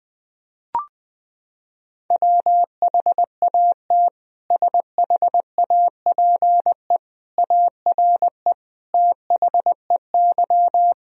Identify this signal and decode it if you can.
Morse code, heard as a keyed tone, about 20 wpm, 700 Hz. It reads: WHAT SHAPE ARE THEY
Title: Sentences from Top 1000 Words - Review Part 3 - 20wpm